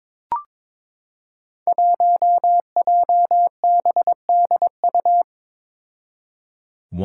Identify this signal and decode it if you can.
Morse code, heard as a keyed tone, about 22 wpm, 700 Hz. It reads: 1JBDU
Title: Five Challenging Letter-Number Combinations 22wpm